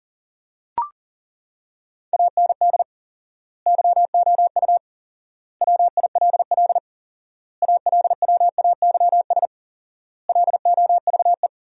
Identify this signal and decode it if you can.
Morse code, heard as a keyed tone, about 37 wpm, 700 Hz. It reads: AND YOU WILL ALWAYS LOVE
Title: 4x Word-Spacing - Sentences from Top 400 Words - 40wpm